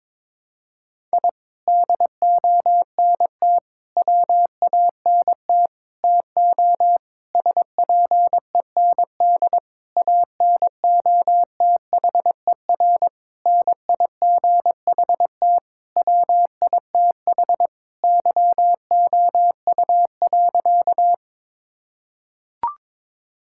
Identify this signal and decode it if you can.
Morse code, heard as a keyed tone, about 22 wpm, 700 Hz. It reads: I DONT WANT TO SPEND ANOTHER NIGHT WITH YOU.
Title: Sentences from Top 1000 Words - Review Part 5 - 22wpm